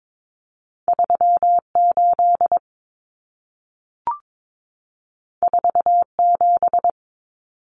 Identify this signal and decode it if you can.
Morse code, heard as a keyed tone, about 22 wpm, 700 Hz. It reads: 38 47
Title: Numbers - 2-Digit 22wpm